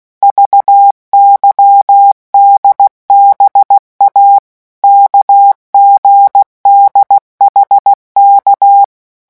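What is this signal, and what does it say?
Morse code, heard as a keyed tone, about 16 wpm, 800 Hz. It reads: VYDBA KGDHK
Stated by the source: Morse letter-group revision drill at 15 words per minute.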